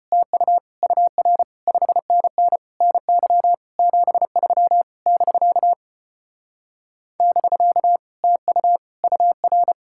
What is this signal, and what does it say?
Morse code, heard as a keyed tone, about 33 wpm, 700 Hz. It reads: TU UR 5NN NY 73 <BK> <BK> TU UR